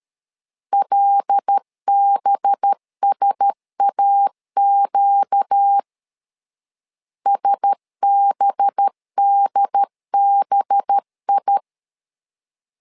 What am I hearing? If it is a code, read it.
Morse code, heard as a keyed tone, about 13 wpm, 800 Hz. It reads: LBSAQ SBDBI
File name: LEZIONI / 5 LBVC / 5 LB 12wpm 1906